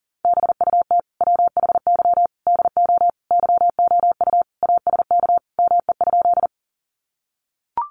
Morse code, heard as a keyed tone, about 40 wpm, 700 Hz. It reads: BUT WHY DO YOU ASK ME?
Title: Sentences from Top 200 Words 40wpm